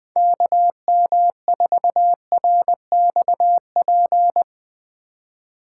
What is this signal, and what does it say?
Morse code, heard as a keyed tone, about 20 wpm, 700 Hz. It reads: KM4RXP
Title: Speed Racing - USA 2x3 Call Signs 20wpm